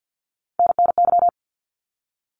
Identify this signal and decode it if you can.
Morse code, heard as a keyed tone, about 48 wpm, 700 Hz. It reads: NNY
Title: ARRL Sections 50wpm